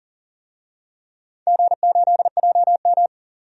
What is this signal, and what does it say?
Morse code, heard as a keyed tone, about 37 wpm, 700 Hz. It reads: G8JM
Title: DXCC-100 Call Signs - 4 Characters - 40wpm